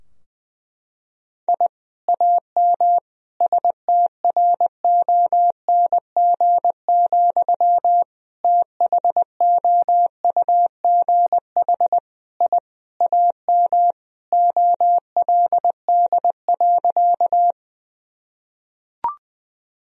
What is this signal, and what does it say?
Morse code, heard as a keyed tone, about 20 wpm, 700 Hz. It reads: I AM STRONG, THOUGH I AM OLD.
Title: Sentences from Top 1000 Words - Review Part 3 - 20wpm